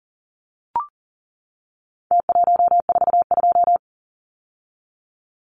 Morse code, heard as a keyed tone, about 40 wpm, 700 Hz. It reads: T142
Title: Numbers - Contest Style - 4-Digit 40wpm